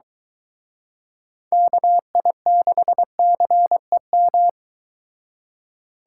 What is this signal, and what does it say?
Morse code, heard as a keyed tone, about 23 wpm, 700 Hz. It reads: KI6CEM